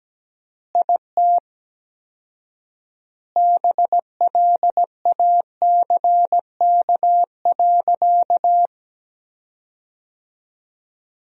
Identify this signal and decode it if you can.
Morse code, heard as a keyed tone, about 17 wpm, 700 Hz. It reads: IT BLACK.
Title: 4x Word-Spacing - Sentences from Top 400 Words - 17wpm